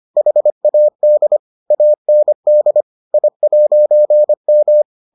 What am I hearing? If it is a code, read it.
Morse code, heard as a keyed tone, about 25 wpm, 600 Hz. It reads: HAD AND I'M